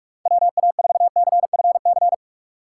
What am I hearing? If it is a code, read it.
Morse code, heard as a keyed tone, about 42 wpm, 700 Hz. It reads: WA4CFC
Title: USA Call Signs - 6 Characters - 4x Character Spacing - 45wpm